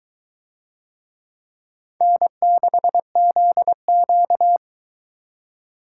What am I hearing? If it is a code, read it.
Morse code, heard as a keyed tone, about 22 wpm, 700 Hz. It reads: N6ZQ